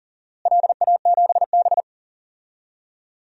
Morse code, heard as a keyed tone, about 40 wpm, 700 Hz. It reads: LA7B
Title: DXCC-100 Call Signs - 4 Characters - Repeated 2x - 40wpm